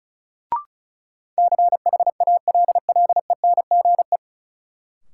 Morse code, heard as a keyed tone, about 34 wpm, 700 Hz. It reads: CHALLENGE
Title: Sentences from Top 1000 Words - Review Part 7 - New Word Drill - 35wpm